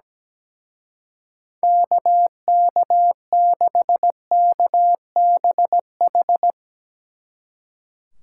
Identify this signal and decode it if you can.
Morse code, heard as a keyed tone, about 17 wpm, 700 Hz. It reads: KK6KBH
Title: Speed Racing - USA 2x3 Call Signs 17wpm